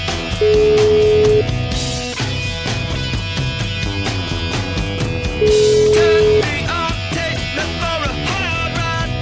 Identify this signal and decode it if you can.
Ringback tone (Europe, 425 Hz), with noise.